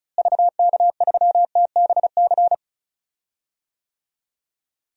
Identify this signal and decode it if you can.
Morse code, heard as a keyed tone, about 36 wpm, 700 Hz. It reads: VK3TBC
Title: DXCC-100 Call Signs - 6 Characters - Repeated 3x - 35wpm